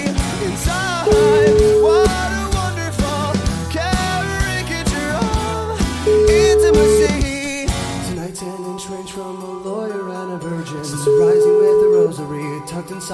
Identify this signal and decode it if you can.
Ringback tone (Europe, 425 Hz), with speech.